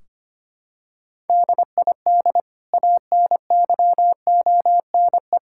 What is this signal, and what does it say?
Morse code, heard as a keyed tone, about 25 wpm, 700 Hz. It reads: DID ANYONE
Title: Sentences from Top 1600 Words 25wpm